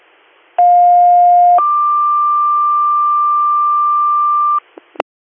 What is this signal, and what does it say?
Two-tone page, heard as a one-second tone, then a three-second tone, about 710 then 1180 Hz, with noise. It is heard from 0.6 to 4.6 s.